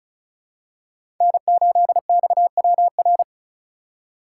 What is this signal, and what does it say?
Morse code, heard as a keyed tone, about 36 wpm, 700 Hz. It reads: N8XWR
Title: USA Call Signs - 5 Characters - 4x Character Spacing - 35wpm